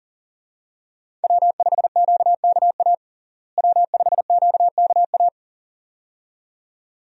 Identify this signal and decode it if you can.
Morse code, heard as a keyed tone, about 40 wpm, 700 Hz. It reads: W5QKA W5QKA
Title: USA Call Signs - 5 Characters - Repeated 2x - 40wpm